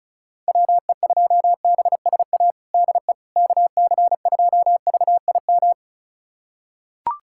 Morse code, heard as a keyed tone, about 36 wpm, 700 Hz. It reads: WE2BSA DE KC2VIM
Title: USA Callsign DE Callsign - 6 Characters - 35wpm